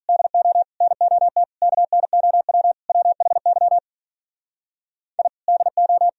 Morse code, heard as a keyed tone, about 44 wpm, 700 Hz. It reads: DO NOT KNOW WHY I DO